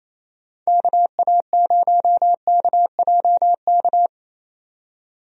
Morse code, heard as a keyed tone, about 28 wpm, 700 Hz. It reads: KA0KJK
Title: USA Call Signs - 6 Characters - 28wpm